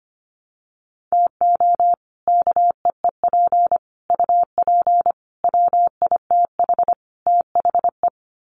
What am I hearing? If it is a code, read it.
Morse code, heard as a keyed tone, about 25 wpm, 700 Hz. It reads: TO KEEP UP WITH THE
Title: Sets of 5 Words Encore 25wpm